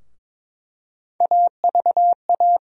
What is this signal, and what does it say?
Morse code, heard as a keyed tone, about 22 wpm, 700 Hz. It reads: AVA
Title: USA Call Sign Postfixes - 3 Letters 22wpm